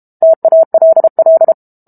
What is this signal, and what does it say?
Morse code, heard as a keyed tone, about 32 wpm, 650 Hz. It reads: TALL
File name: Book Two / 06-Chapter 6 - 32 WPM